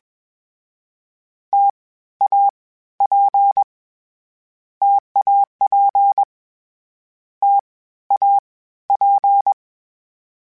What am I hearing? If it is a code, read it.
Morse code, heard as a keyed tone, about 21 wpm, 800 Hz. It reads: T A P TAP T A P
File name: Difficult Combos / Difficult Combos-mp3s / T AOEI @20